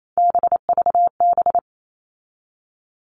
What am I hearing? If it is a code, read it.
Morse code, heard as a keyed tone, about 28 wpm, 700 Hz. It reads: BVB
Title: All Cut Numbers - 3-Digit 28wpm